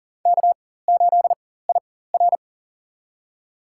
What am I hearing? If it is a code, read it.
Morse code, heard as a keyed tone, about 40 wpm, spaced as for 19 wpm, 700 Hz. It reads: K8IR
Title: USA Call Signs - 4 Characters - 4x Character Spacing - 40wpm